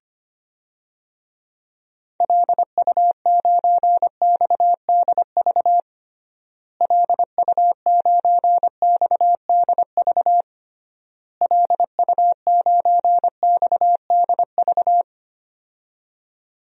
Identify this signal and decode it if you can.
Morse code, heard as a keyed tone, about 25 wpm, 700 Hz. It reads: LU9XDV LU9XDV LU9XDV
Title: DXCC-100 Call Signs - 6 Characters - Repeated 3x - 25wpm